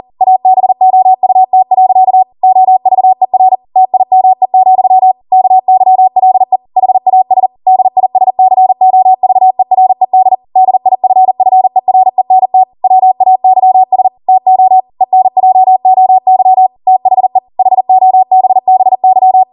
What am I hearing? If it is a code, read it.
Morse code, heard as a keyed tone, about 40 wpm, 750 Hz. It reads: ABOUT. OVER TIME, KYLE HAS DISCOVERED DIFFERENT WAYS TO ENJOY THE HOBBY